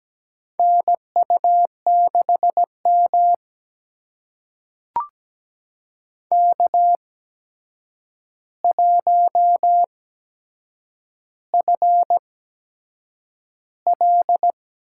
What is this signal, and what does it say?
Morse code, heard as a keyed tone, about 17 wpm, spaced as for 7 wpm, 700 Hz. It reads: NU6M K1FL
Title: USA Call Signs - 4 Characters - 8x Character Spacing - 17wpm